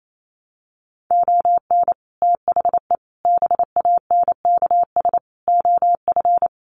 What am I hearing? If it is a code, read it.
Morse code, heard as a keyed tone, about 28 wpm, 700 Hz. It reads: ON THE BANKS OF
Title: Sets of 5 Words Encore2 28wpm